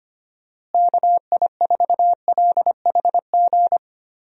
Morse code, heard as a keyed tone, about 25 wpm, 700 Hz. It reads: KI4LHG
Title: USA Call Signs - 6 Characters - Repeated 2x - 25wpm